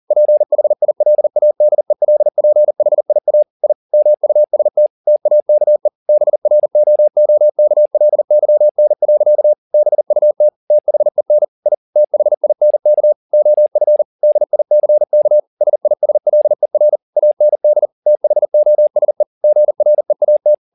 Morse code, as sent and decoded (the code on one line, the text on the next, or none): .--. .... .. .-.. .- -.. .-.. .--. .... .. .- / .. / -- ..- ... - / - .- -.- . / -... .-. --- --- -.- .-.. -.-- -. .-.-.- / -... ..- - / - .... -. / .. / - .... .. -. -.- / --- ..-. / -.. .. -.-. -.- / ... .. ... .-.. .-. / .- -. -.. / - .... --- ... / --. .-. .- -
PHILADELPHIA I MUST TAKE BROOKLYN. BUT THEN I THINK OF DICK SISLER AND THOSE GREAT